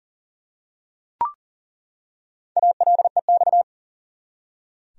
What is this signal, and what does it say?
Morse code, heard as a keyed tone, about 40 wpm, 700 Hz. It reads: ALEX
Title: USA Common Names - Max 4 Letters - 40wpm